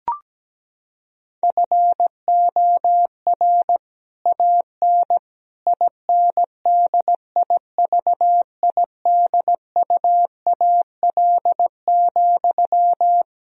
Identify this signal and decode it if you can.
Morse code, heard as a keyed tone, about 17 wpm, 700 Hz. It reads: FOR AN INDIVIDUAL,
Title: Sentences from Top 1000 Words - Review Part 4 - 17wpm